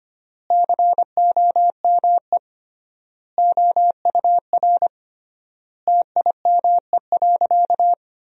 Morse code, heard as a keyed tone, about 25 wpm, 700 Hz. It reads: COME OUR TIME.
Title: 3x Word-Spacing - Sentences from Top 200 Words - 25wpm